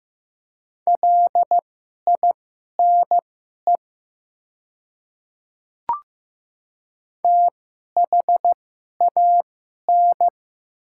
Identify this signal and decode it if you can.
Morse code, heard as a keyed tone, about 15 wpm, spaced as for 11 wpm, 700 Hz. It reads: LINE THAN
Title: 2x Character Spacing - Top 200 Words - 15wpm